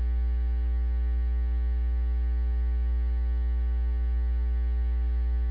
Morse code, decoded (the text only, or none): none